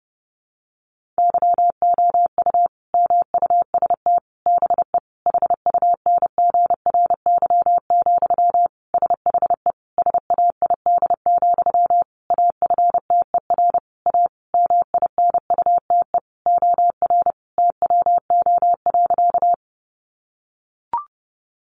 Morse code, heard as a keyed tone, about 30 wpm, 700 Hz. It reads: YOU MUST BE HUNGRY, SHE SAID, AFTER A MINUTE OR TWO.
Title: Sentences from Top 1400 Words 30wpm